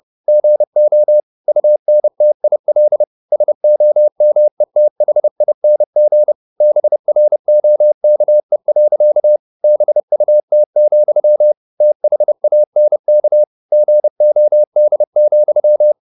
Morse code, heard as a keyed tone, about 30 wpm, 600 Hz. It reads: GO UNTIL SOMETHING BROKE. BUT, THANK GOD,
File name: the old man and the sea / the old man and the sea 1 30wpm